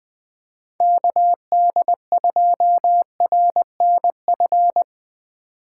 Morse code, heard as a keyed tone, about 20 wpm, 700 Hz. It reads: KD2RNF
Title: USA Call Signs - 6 Characters - 3x Character Spacing - 20wpm